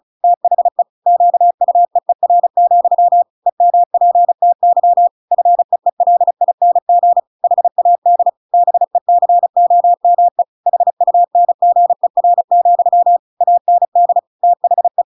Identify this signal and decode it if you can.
Morse code, heard as a keyed tone, about 34 wpm, 700 Hz. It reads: THE QUEER, EMPTY FEELING HAD BECOME HUNGER, AND THE
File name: Pinocchio / Pinocchio ch5 35 35 ik1hge